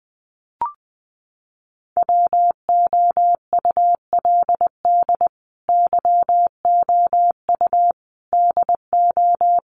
Morse code, heard as a keyed tone, about 20 wpm, 700 Hz. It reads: WOULD YOU DO
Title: Sentences from Top 100 Words 20wpm